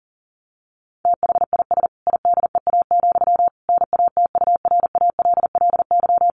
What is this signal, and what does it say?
Morse code, heard as a keyed tone, about 40 wpm, 700 Hz. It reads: THIS IDEA, NATURALLY
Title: Sentences from Top 1300 Words 40wpm